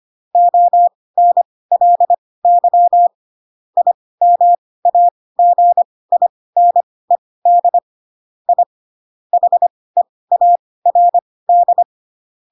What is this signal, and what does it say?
Morse code, heard as a keyed tone, about 25 wpm, spaced as for 18 wpm, 700 Hz. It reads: ONLY IMAGINED I HEARD
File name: Pinocchio / Pinocchio ch1 12 25 ik1hge